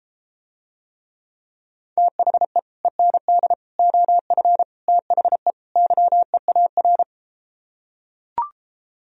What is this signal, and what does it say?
Morse code, heard as a keyed tone, about 33 wpm, 700 Hz. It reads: THE END OF THE YEAR